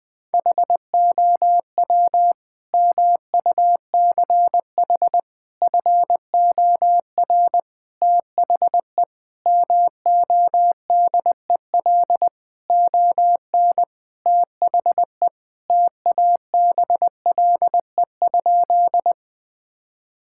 Morse code, heard as a keyed tone, about 20 wpm, 700 Hz. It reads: HOW MUCH FOR THE MODEL ON THE TABLE?
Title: Sentences from Top 1000 Words - Review Part 5 - 20wpm